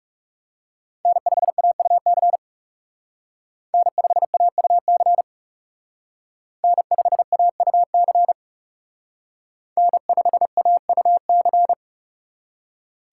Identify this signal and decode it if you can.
Morse code, heard as a keyed tone, about 38 wpm, 700 Hz. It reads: N5AUC N5AUC N5AUC N5AUC